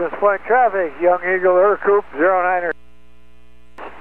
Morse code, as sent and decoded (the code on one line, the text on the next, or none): none